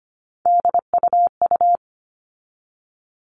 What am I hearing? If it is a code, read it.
Morse code, heard as a keyed tone, about 25 wpm, 700 Hz. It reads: DUU
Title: All Cut Numbers - 3-Digit 25wpm